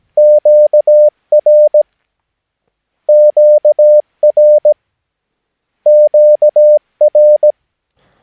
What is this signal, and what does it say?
Morse code, heard as a keyed tone, about 17 wpm, 600 Hz. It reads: QR QR QR
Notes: Övningsmaterial med alfabetet A till Ö samt siffror 0 till 9 enligt International/Continental Morse Code.